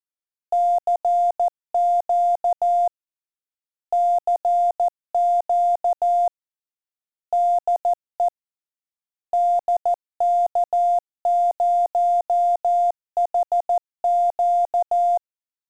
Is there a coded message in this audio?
Yes: Morse code from 0.5 to 15.2 s, heard as a keyed tone, about 14 wpm, 700 Hz. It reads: CQ CQ DE DK0HQ